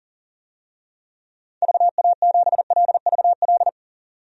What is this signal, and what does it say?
Morse code, heard as a keyed tone, about 40 wpm, 700 Hz. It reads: VA7LVL
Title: DXCC-100 Call Signs - 6 Characters - 40wpm